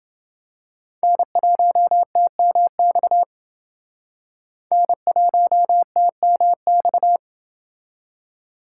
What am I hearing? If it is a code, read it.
Morse code, heard as a keyed tone, about 27 wpm, 700 Hz. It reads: N1TMX N1TMX